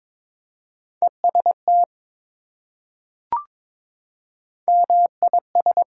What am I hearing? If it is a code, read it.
Morse code, heard as a keyed tone, about 22 wpm, 700 Hz. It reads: EST MIS
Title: Prefixes and Suffixes 22wpm